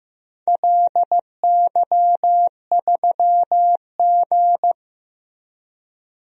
Morse code, heard as a keyed tone, about 16 wpm, 700 Hz. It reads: LY3G
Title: DXCC-100 Call Signs - 4 Characters - Repeated 2x - 15wpm